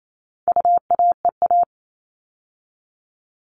Morse code, heard as a keyed tone, about 28 wpm, 700 Hz. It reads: UAEA
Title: All Cut Numbers - 4-Digit 28wpm